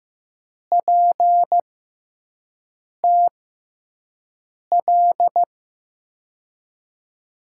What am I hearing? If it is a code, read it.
Morse code, heard as a keyed tone, about 15 wpm, spaced as for 5 wpm, 700 Hz. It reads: PTL
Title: USA Call Signs - 5 Characters - 6x Character Spacing - 15wpm